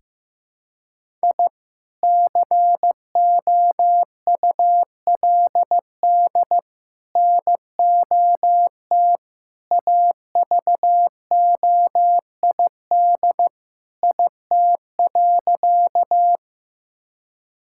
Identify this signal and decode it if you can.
Morse code, heard as a keyed tone, about 16 wpm, 700 Hz. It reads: I COULD NOT AVOID IT.
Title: Sentences from Top 1000 Words - Review Part 6 - 15wpm